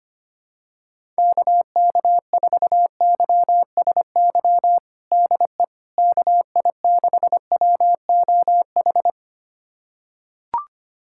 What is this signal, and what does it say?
Morse code, heard as a keyed tone, about 25 wpm, 700 Hz. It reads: KK4YSY DE KI6WOH
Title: USA Callsign DE Callsign 25wpm